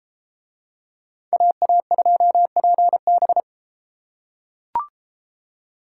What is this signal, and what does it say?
Morse code, heard as a keyed tone, about 33 wpm, 700 Hz. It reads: AA2PB